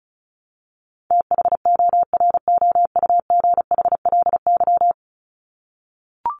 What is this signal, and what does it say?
Morse code, heard as a keyed tone, about 34 wpm, 700 Hz. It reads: THOROUGHLY
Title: LY Suffix - Top 100 Words with Base Words 35wpm